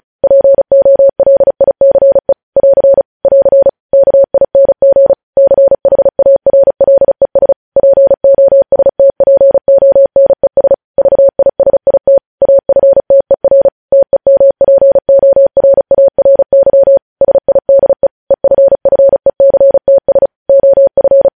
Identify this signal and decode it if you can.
Morse code, heard as a keyed tone, about 36 wpm, 550 Hz. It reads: POLICE <AR> <AR> KING CHARLES POSTPONES VISIT AFTER TEMPORARY SIDE EFFECTS OF
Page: News headlines in Morse code at 35 wpm